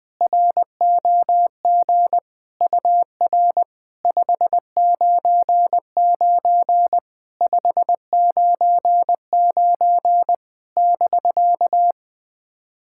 Morse code, heard as a keyed tone, about 20 wpm, 700 Hz. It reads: ROG UR 599 599 <BK>
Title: SOTA QSO Practice 20wpm